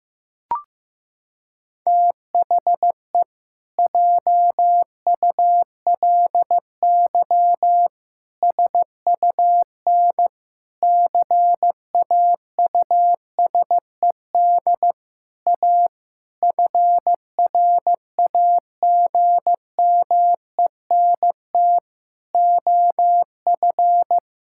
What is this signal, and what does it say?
Morse code, heard as a keyed tone, about 15 wpm, 700 Hz. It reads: THE JULY SUN CAUSED A FRAGMENT OF
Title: Pangram Challenge v2 - 15wpm